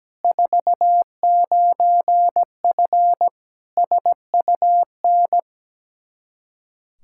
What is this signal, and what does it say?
Morse code, heard as a keyed tone, about 17 wpm, 700 Hz. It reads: 49F SUN
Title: Practice QSO - Exchange 2 - Info - 17wpm